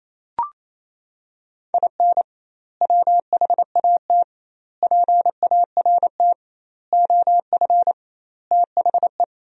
Morse code, heard as a keyed tone, about 28 wpm, 700 Hz. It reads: IN WHAT PART OF THE
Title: 2x Word-Spacing - Sentences from Top 200 Words - 28wpm